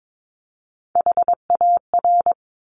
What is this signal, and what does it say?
Morse code, heard as a keyed tone, about 22 wpm, 700 Hz. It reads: HAR
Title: Top Three Letter Combinations in Words 22wpm